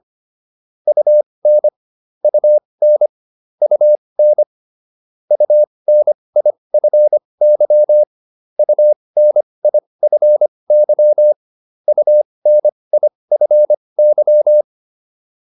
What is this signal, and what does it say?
Morse code, heard as a keyed tone, about 25 wpm, spaced as for 20 wpm, 600 Hz. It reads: UN UN UN UNIFY UNIFY UNIFY